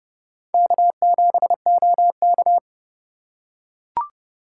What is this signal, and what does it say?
Morse code, heard as a keyed tone, about 29 wpm, 700 Hz. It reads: K7OK